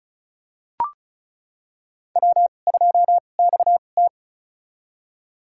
Morse code, heard as a keyed tone, about 36 wpm, spaced as for 25 wpm, 700 Hz. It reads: W2XT
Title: USA Call Signs - 4 Characters - 2x Character Spacing - 35wpm